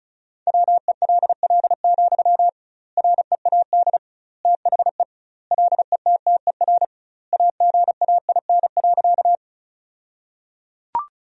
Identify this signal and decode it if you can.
Morse code, heard as a keyed tone, about 34 wpm, 700 Hz. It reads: WELL, READ THE LETTER AGAIN.
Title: 2x Word-Spacing - Sentences from Top 300 Words - 35wpm